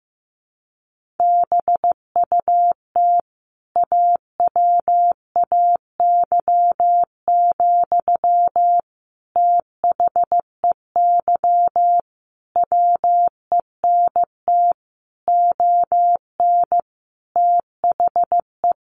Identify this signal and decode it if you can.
Morse code, heard as a keyed tone, about 15 wpm, 700 Hz. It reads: BUT AWAY, THEY WENT ON THE
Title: Sentences from Top 1400 Words 15wpm